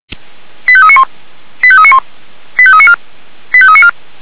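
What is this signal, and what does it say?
Five-tone selective call (ZVEI), heard as quick runs of five tones, tones of about 70 ms. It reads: 85391 85391 85394 85394
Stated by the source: ZVEI five-tone call received on an SDR.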